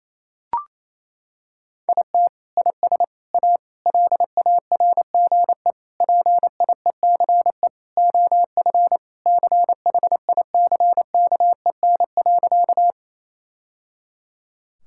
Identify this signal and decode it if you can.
Morse code, heard as a keyed tone, about 28 wpm, 700 Hz. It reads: IT IS A LARGE PIECE OF CHICKEN.
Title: Sentences from Top 1000 Words - Review Part 5 - 28wpm